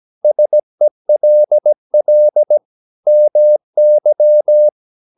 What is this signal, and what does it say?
Morse code, heard as a keyed tone, about 17 wpm, 600 Hz. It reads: SELL MY
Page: Romanen Never af Ken Follet på engelsk.